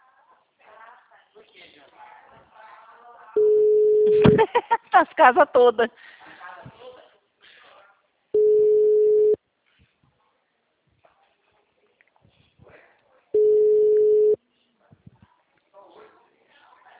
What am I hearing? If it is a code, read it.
Ringback tone (Europe, 425 Hz), with speech.
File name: out- -2001- - - .19605